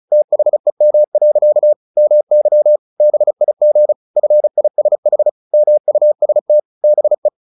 Morse code, heard as a keyed tone, about 35 wpm, 600 Hz. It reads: THEM. MY BIG FISH MUST BE
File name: the old man and the sea / the old man and the sea 1 35wpm